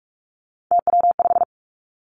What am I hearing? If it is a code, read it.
Morse code, heard as a keyed tone, about 46 wpm, 700 Hz. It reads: TW5
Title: Three Challenging Letter-Number Combinations 45wpm